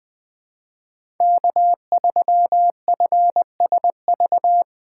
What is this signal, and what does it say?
Morse code, heard as a keyed tone, about 20 wpm, 700 Hz. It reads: K3FSV
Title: USA Call Signs - 5 Characters - 3x Character Spacing - 20wpm